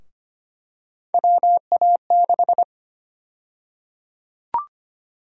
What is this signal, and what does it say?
Morse code, heard as a keyed tone, about 25 wpm, 700 Hz. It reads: WA6
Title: USA Call Sign Prefixes - 25wpm